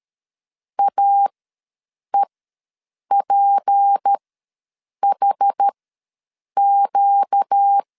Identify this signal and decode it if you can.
Morse code, heard as a keyed tone, about 13 wpm, spaced as for 7 wpm, 800 Hz. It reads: AEPHQ
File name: LEZIONI / 6 PKX0 / 6 P ws 12wpm 3117